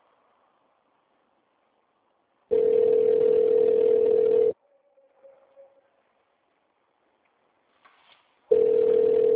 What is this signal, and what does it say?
Ringback tone (North America), 440 + 480 Hz.
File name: out- -2002- - - .18